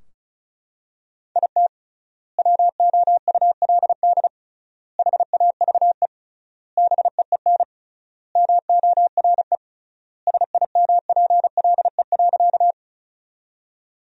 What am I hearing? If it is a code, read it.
Morse code, heard as a keyed tone, about 34 wpm, 700 Hz. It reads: IT WOULD HAVE BEEN MORE SIMPLE.
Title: 3x Word-Spacing - Sentences from Top 400 Words - 35wpm